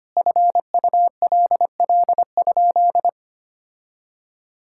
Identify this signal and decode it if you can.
Morse code, heard as a keyed tone, about 25 wpm, 700 Hz. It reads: FULL?
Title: Sentences from Top 1600 Words 25wpm